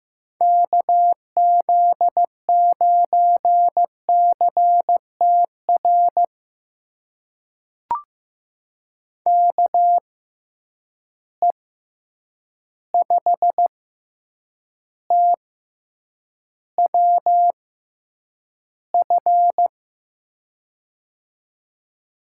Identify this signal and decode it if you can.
Morse code, heard as a keyed tone, about 15 wpm, spaced as for 8 wpm, 700 Hz. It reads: KZ9CTR KE5TWF
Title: USA Call Signs - 6 Characters - 6x Character Spacing - 15wpm